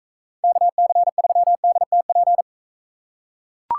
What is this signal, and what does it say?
Morse code, heard as a keyed tone, about 43 wpm, 700 Hz. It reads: KC3DTP